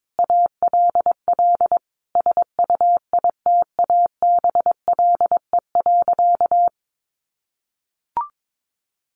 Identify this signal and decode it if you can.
Morse code, heard as a keyed tone, about 22 wpm, 700 Hz. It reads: ALL SUITABLE.
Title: Sentences from Top 1400 Words 22wpm